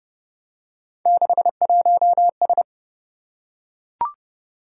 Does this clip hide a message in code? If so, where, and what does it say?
Yes: Morse code from 1.1 to 2.6 s, heard as a keyed tone, about 25 wpm, 700 Hz. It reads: 61S